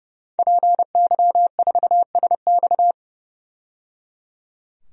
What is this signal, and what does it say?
Morse code, heard as a keyed tone, about 30 wpm, 700 Hz. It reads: PY4SX
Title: DXCC-100 Call Signs - 5 Characters - 30wpm